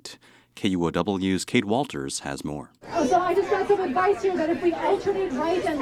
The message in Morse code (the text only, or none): none